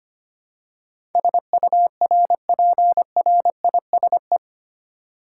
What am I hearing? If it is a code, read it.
Morse code, heard as a keyed tone, about 25 wpm, 700 Hz. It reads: SURPRISE
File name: practice / 25wpm / 8 Letter Words - First 3 Letters Given - 25wpm